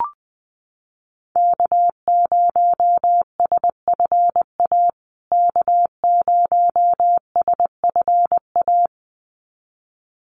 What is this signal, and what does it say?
Morse code, heard as a keyed tone, about 20 wpm, 700 Hz. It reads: K0SFA K0SFA
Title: USA Call Signs Repeated 2x 20wpm